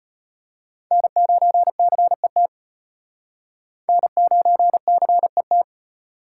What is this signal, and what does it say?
Morse code, heard as a keyed tone, about 34 wpm, 700 Hz. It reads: N9CET N9CET